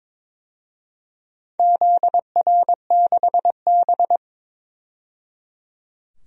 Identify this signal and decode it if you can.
Morse code, heard as a keyed tone, about 23 wpm, 700 Hz. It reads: ZR6B
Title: DXCC-100 Call Signs - 4 Characters - 22wpm